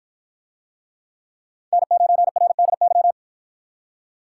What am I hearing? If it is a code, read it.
Morse code, heard as a keyed tone, about 49 wpm, 700 Hz. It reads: N9RDY